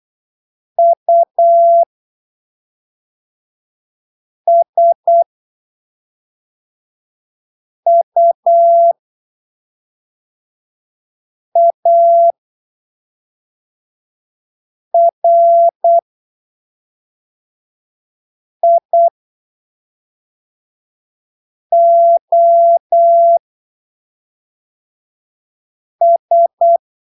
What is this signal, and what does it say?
Morse code, heard as a keyed tone, about 8 wpm, spaced as for 3 wpm, 680 Hz. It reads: USUARIOS
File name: arquivos / texto / texto 06 08ppm